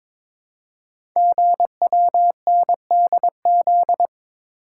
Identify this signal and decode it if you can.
Morse code, heard as a keyed tone, about 23 wpm, 700 Hz. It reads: GWNDZ
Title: Code Groups - 5 Letters - 22wpm